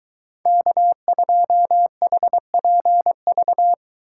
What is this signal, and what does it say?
Morse code, heard as a keyed tone, about 23 wpm, 700 Hz. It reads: K2HPV